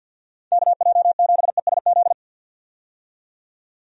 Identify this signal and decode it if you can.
Morse code, heard as a keyed tone, about 52 wpm, 700 Hz. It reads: KJ7ESZ
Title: USA Call Signs - 6 Characters - 50wpm